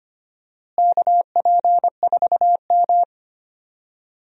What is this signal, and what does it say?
Morse code, heard as a keyed tone, about 25 wpm, 700 Hz. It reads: KP4M
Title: DXCC-100 Call Signs - 4 Characters - Repeated 2x - 25wpm